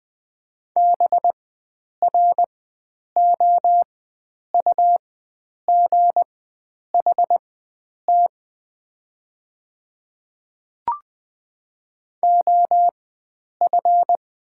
Morse code, heard as a keyed tone, about 20 wpm, spaced as for 9 wpm, 700 Hz. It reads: BROUGHT OF
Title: 4x Character Spacing - Top 500 Words - 20wpm